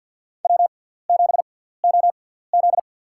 Morse code, heard as a keyed tone, about 52 wpm, spaced as for 17 wpm, 700 Hz. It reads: W7OZ
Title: USA Call Signs - 4 Characters - 6x Character Spacing - 50wpm